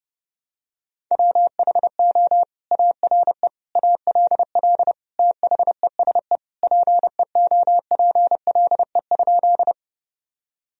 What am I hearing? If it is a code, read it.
Morse code, heard as a keyed tone, about 32 wpm, 700 Hz. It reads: WHO ARE ALL THESE PEOPLE?
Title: Sentences from Top 1000 Words - Review Part 1 - 30wpm